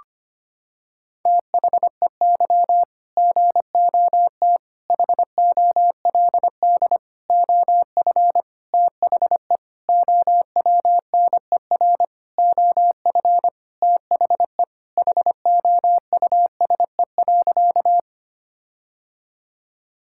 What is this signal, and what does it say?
Morse code, heard as a keyed tone, about 25 wpm, 700 Hz. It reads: THEY GOT HOLD OF THE OWNER OF THE HOUSE.
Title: Sentences from Top 1600 Words 25wpm